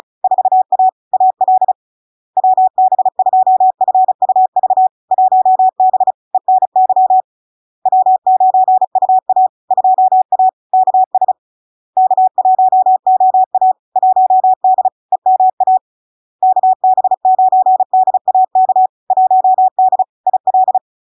Morse code, heard as a keyed tone, about 35 wpm, 750 Hz. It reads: 4A AL WB2FUV 1B ENY W9UA 2A KS K1OA 1D EMA KB9DAK 1D IL
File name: LARK-FD-2022-Log-in-CW-35wpm-750-Hz